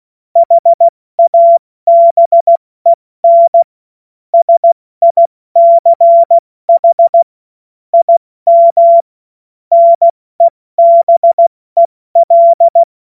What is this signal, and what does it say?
Morse code, heard as a keyed tone, about 16 wpm, 680 Hz. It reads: HABEN SICH IM NEBEL
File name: Morsekurs / Witze / Witz 03. Mai 2022 70 F80